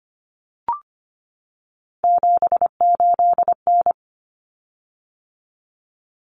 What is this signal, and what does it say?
Morse code, heard as a keyed tone, about 25 wpm, 700 Hz. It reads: 78N
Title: Common Cut Numbers - 3-Digit 25wpm